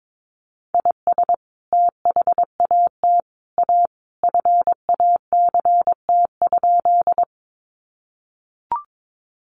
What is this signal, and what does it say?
Morse code, heard as a keyed tone, about 22 wpm, 700 Hz. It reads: IS THAT A FACT?
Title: Sentences from Top 500 Words 22wpm